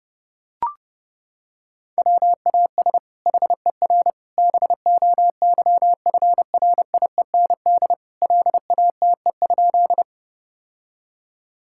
Morse code, heard as a keyed tone, about 30 wpm, 700 Hz. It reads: WAS HER BOYFRIEND LATE?
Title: Sentences from Top 1000 Words - Review Part 5 - 30wpm